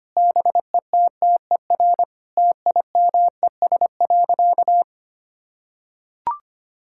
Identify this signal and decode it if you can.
Morse code, heard as a keyed tone, about 25 wpm, 700 Hz. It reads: BETTER TIMES.
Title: Sentences from Top 1000 Words - Review Part 6 - 25wpm